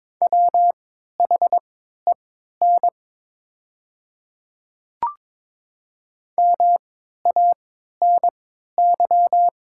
Morse code, heard as a keyed tone, about 22 wpm, spaced as for 13 wpm, 700 Hz. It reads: WHEN MANY
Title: 3x Character Spacing - Top 300 Words - 22wpm